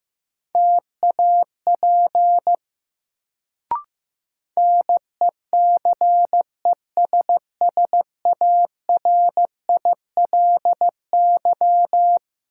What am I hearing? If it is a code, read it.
Morse code, heard as a keyed tone, about 15 wpm, 700 Hz. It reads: TAP NECESSARILY
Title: Sentences from Top 1400 Words - New Word Drill Alt - 15wpm